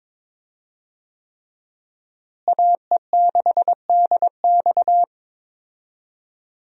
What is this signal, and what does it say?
Morse code, heard as a keyed tone, about 22 wpm, 700 Hz. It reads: AE6DX
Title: USA Call Signs - 5 Characters - Repeated 3x - 22wpm